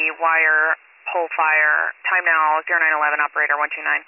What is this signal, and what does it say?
Emergency Alert System (SAME) tones, heard as harsh buzzing bursts, with speech. It is heard from 0.4 to 3.5 s.